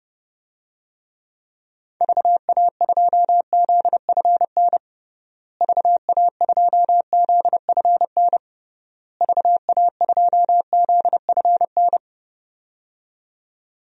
Morse code, heard as a keyed tone, about 30 wpm, 700 Hz. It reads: VA2ZFN VA2ZFN VA2ZFN
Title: DXCC-100 Call Signs - 6 Characters - Repeated 3x - 30wpm